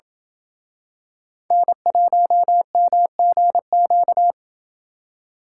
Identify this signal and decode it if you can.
Morse code, heard as a keyed tone, about 28 wpm, 700 Hz. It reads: N1MGQ